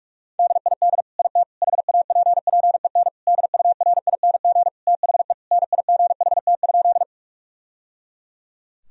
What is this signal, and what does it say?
Morse code, heard as a keyed tone, about 42 wpm, 700 Hz. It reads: DID IT HAPPEN DURING THE NIGHT?
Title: Sentences from Top 1000 Words - Review Part 2 - 45wpm